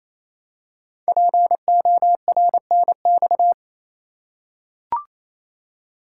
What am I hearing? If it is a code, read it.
Morse code, heard as a keyed tone, about 27 wpm, 700 Hz. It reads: PORNX
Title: Code Groups - 5 Letters - 28wpm